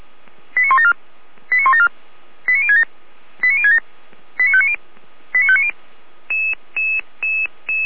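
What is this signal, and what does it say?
Five-tone selective call (ZVEI), heard as quick runs of five tones, tones of about 70 ms. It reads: 89174 89174 89067 89067 89590 89590
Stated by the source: ZVEI five-tone selective call received on an SDR.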